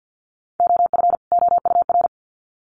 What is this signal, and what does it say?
Morse code, heard as a keyed tone, about 50 wpm, 700 Hz. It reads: OF OUR
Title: Sets of 2 Words 50wpm